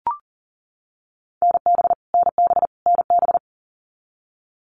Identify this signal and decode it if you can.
Morse code, heard as a keyed tone, about 40 wpm, 700 Hz. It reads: NB NB NB
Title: Canadian Provinces and Territories - Repeated 3x - 40wpm